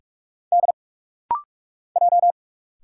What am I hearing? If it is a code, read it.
Morse code, heard as a keyed tone, about 46 wpm, 700 Hz. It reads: D J
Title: Single Letters - Rapid-Fire 45wpm